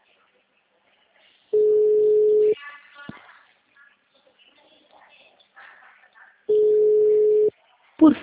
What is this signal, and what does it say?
Ringback tone (Europe, 425 Hz), with speech.
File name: out- -2012- - - .1786